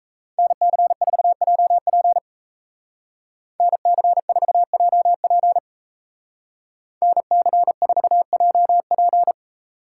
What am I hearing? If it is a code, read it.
Morse code, heard as a keyed tone, about 33 wpm, 700 Hz. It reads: NC4JP NC4JP NC4JP
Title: Speed Racing - USA 2x2 Call Signs 28wpm